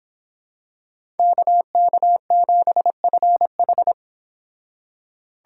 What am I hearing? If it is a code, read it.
Morse code, heard as a keyed tone, about 25 wpm, 700 Hz. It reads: KK7FH